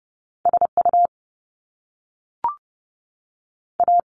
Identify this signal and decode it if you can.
Morse code, heard as a keyed tone, about 30 wpm, 700 Hz. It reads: S U A
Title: Top Two Letter Combinations in Words 30wpm